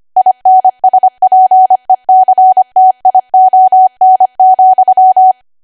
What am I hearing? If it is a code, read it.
Morse code, heard as a keyed tone, about 25 wpm, 750 Hz. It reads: INSPECTION,
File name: Archive / 25 WPM / 25WPM